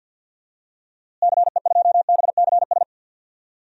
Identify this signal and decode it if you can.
Morse code, heard as a keyed tone, about 53 wpm, 700 Hz. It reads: CE2BCS
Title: DXCC-100 Call Signs - 6 Characters - Repeated 2x - 50wpm